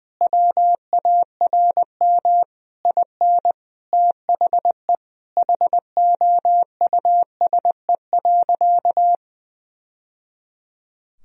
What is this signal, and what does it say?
Morse code, heard as a keyed tone, about 20 wpm, 700 Hz. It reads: WARM IN THE HOUSE.
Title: Sentences from Top 1000 Words - Review Part 4 - 20wpm